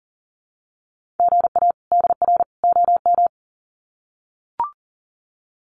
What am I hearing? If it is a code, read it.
Morse code, heard as a keyed tone, about 40 wpm, 700 Hz. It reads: GA DR OM
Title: Common QSO Phrases 40wpm